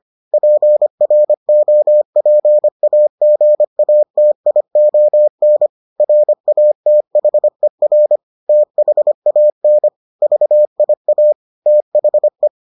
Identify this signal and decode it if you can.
Morse code, heard as a keyed tone, about 25 wpm, 600 Hz. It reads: PROPAGATION RATHER THAN VIA THE